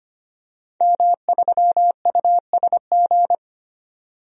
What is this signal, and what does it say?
Morse code, heard as a keyed tone, about 25 wpm, 700 Hz. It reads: M3USG
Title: DXCC-100 Call Signs - 5 Characters - 25wpm